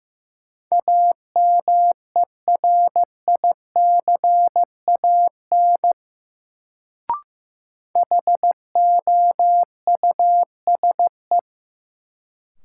Morse code, heard as a keyed tone, about 15 wpm, 700 Hz. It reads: AMERICAN HOUSE
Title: Sets of 2 Words - New Word Drill - 15wpm